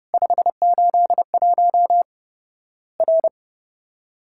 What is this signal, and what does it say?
Morse code, heard as a keyed tone, about 30 wpm, 700 Hz. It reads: 581 R
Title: Worked All Europe DX Contest - QTC Exchange - 30wpm